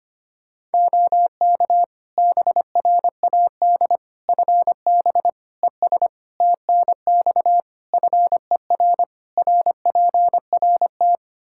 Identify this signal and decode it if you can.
Morse code, heard as a keyed tone, about 25 wpm, 700 Hz. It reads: OK BRAD FB ES TNX FER RPRT
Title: K1USN SST QSO Practice 25wpm